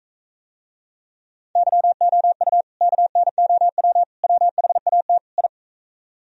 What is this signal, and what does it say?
Morse code, heard as a keyed tone, about 45 wpm, 700 Hz. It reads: YOU KNOW WHAT I